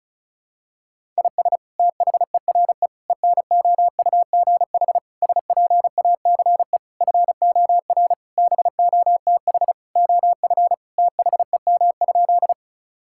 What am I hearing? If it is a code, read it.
Morse code, heard as a keyed tone, about 34 wpm, 700 Hz. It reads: IS THERE ENOUGH SPACE FOR BOTH OF THEM?
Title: Sentences from Top 1000 Words - Review Part 5 - 35wpm